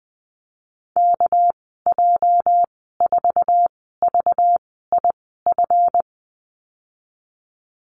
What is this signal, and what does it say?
Morse code, heard as a keyed tone, about 19 wpm, spaced as for 14 wpm, 700 Hz. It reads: KJ4VIF
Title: 245 Lesson 39g - Letter z - Call Signs 20wpm